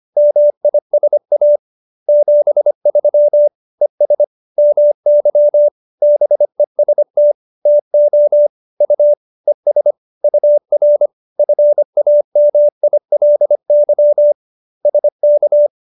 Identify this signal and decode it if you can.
Morse code, heard as a keyed tone, about 25 wpm, 590 Hz. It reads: MISA 73 ES MY BEST TO U ES UR FAMILY SK